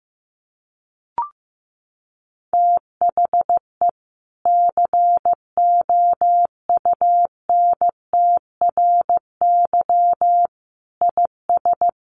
Morse code, heard as a keyed tone, about 15 wpm, 700 Hz. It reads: THE COUNTRY IS
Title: Sentences from Top 1100 Words 15wpm